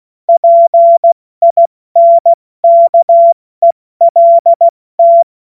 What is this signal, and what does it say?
Morse code, heard as a keyed tone, about 16 wpm, 680 Hz. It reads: PINKELT
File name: Morsekurs / Witze / Witz 17 Juni Mai 2025 70 F80